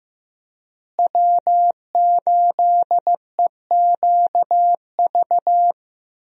Morse code, heard as a keyed tone, about 15 wpm, 700 Hz. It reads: W8EQV
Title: USA Call Signs - 5 Characters - Repeated 3x - 15wpm